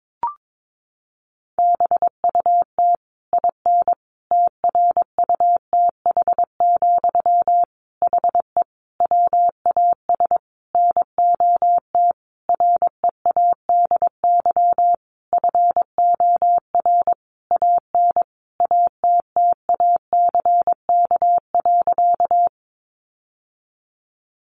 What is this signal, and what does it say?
Morse code, heard as a keyed tone, about 22 wpm, 700 Hz. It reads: BUT IN TRUTH, HE WAS NOT READY FOR AN ATTACK.
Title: Sentences from Top 1500 Words 22wpm